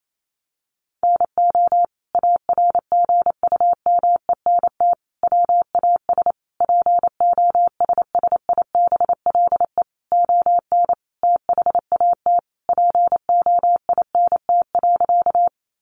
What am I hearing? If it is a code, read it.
Morse code, heard as a keyed tone, about 28 wpm, 700 Hz. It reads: NO ARGUMENT WAS POSSIBLE ON THAT POINT.
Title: Sentences from Top 1400 Words 28wpm